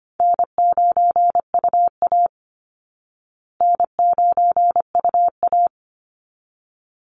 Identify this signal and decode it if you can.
Morse code, heard as a keyed tone, about 25 wpm, 700 Hz. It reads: N9UA N9UA
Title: Call Signs Repeated 3x - 25wpm